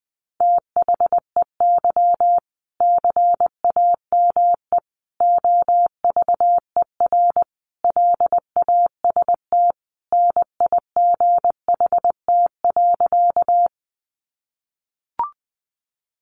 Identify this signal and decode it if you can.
Morse code, heard as a keyed tone, about 20 wpm, 700 Hz. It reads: THEY CAME OVER LAST NIGHT.
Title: Sentences from Top 300 Words 20wpm